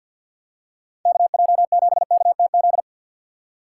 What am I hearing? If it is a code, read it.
Morse code, heard as a keyed tone, about 51 wpm, 700 Hz. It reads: KJ7KTZ
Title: USA Call Signs - 6 Characters - 4x Character Spacing - 50wpm